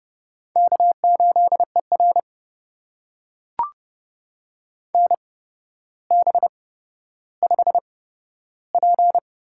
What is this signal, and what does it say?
Morse code, heard as a keyed tone, about 30 wpm, spaced as for 13 wpm, 700 Hz. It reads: K8ER NB5P
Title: USA Call Signs - 4 Characters - 8x Character Spacing - 30wpm